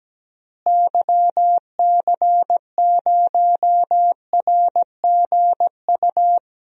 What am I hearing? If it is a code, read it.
Morse code, heard as a keyed tone, about 17 wpm, 700 Hz. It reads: YC0RGU
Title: DXCC-100 Call Signs - 6 Characters - Repeated 2x - 17wpm